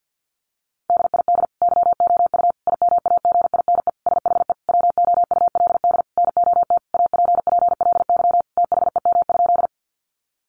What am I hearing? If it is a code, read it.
Morse code, heard as a keyed tone, about 48 wpm, 700 Hz. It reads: DID YOU IMAGINE SHE WOULD NOT APPLY THEM?
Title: Sentences from Top 1400 Words 50wpm